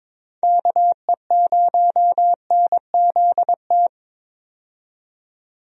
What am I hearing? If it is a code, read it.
Morse code, heard as a keyed tone, about 21 wpm, 700 Hz. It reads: KE0NZT
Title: USA Call Signs - 6 Characters - 22wpm